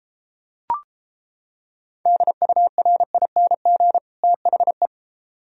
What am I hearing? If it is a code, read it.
Morse code, heard as a keyed tone, about 29 wpm, 700 Hz. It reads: DURING THE